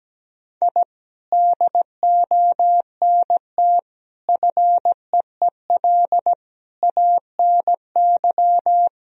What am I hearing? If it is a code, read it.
Morse code, heard as a keyed tone, about 17 wpm, 700 Hz. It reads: I DONT FEEL ANY
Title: Sentences from Top 1000 Words - Review Part 3 - 17wpm